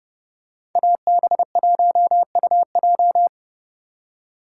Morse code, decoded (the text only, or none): AB1UJ